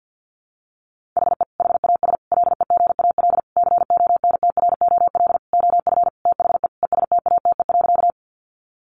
Morse code, heard as a keyed tone, about 50 wpm, 700 Hz. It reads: HE HAS LEGAL CONTROL OF THE ESTATE.
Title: Sentences from Top 1200 Words 50wpm